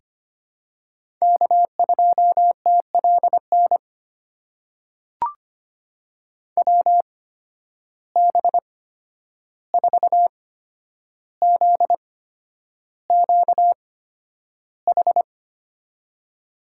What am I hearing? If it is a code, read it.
Morse code, heard as a keyed tone, about 25 wpm, spaced as for 10 wpm, 700 Hz. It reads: K2TLN WB4ZQH
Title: USA Call Signs - 8x Character Spacing - 25wpm